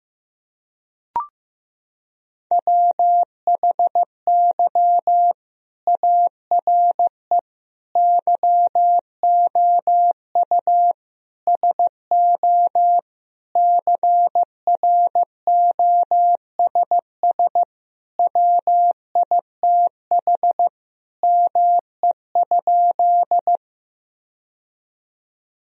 Morse code, heard as a keyed tone, about 15 wpm, 700 Hz. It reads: WHY ARE YOU SO CROSS WITH ME?
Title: Sentences from Top 1000 Words - Review Part 1 - 15wpm